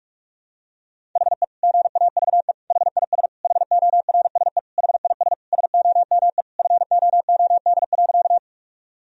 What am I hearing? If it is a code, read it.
Morse code, heard as a keyed tone, about 41 wpm, 700 Hz. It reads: HE GAVE HIS HORSE HIS SOME FOOD.
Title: Sentences from Top 1000 Words - Review Part 1 - 45wpm